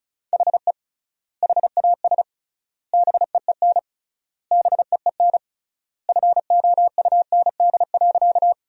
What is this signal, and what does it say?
Morse code, heard as a keyed tone, about 36 wpm, 700 Hz. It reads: HE HAS BEEN BEEN FOUND.
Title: 3x Word-Spacing - Sentences from Top 300 Words - 35wpm